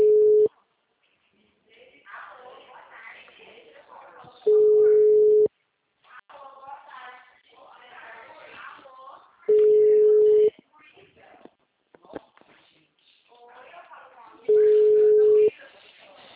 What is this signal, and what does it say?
Ringback tone (Europe, 425 Hz), with speech.
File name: out- -2013- - - .74142